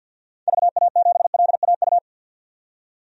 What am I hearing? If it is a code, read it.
Morse code, heard as a keyed tone, about 52 wpm, 700 Hz. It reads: VA7LAU
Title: DXCC-100 Call Signs - 6 Characters - 50wpm